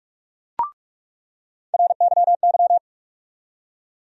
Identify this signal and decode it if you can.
Morse code, heard as a keyed tone, about 46 wpm, 700 Hz. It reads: RYY
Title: USA Call Sign Postfixes - 3 Letters 45wpm